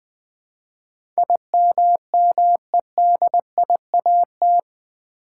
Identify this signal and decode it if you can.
Morse code, heard as a keyed tone, about 20 wpm, 700 Hz. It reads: IMMEDIAT
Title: Build A Word - 4 Letters At A Time - 11-Letter Words - 20wpm